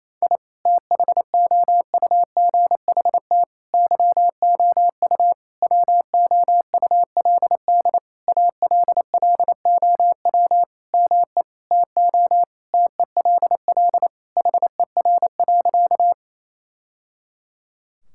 Morse code, heard as a keyed tone, about 27 wpm, 700 Hz. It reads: I THOUGHT YOU WOULD ALLOW ME TO TELL HER.
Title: Sentences from Top 1000 Words - Review Part 5 - 28wpm